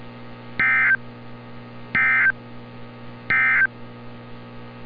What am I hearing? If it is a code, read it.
Emergency Alert System (SAME) tones, heard as harsh buzzing bursts. It is heard from 0.5 to 3.7 s.
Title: NOAA Weather Radio - Altoona, PA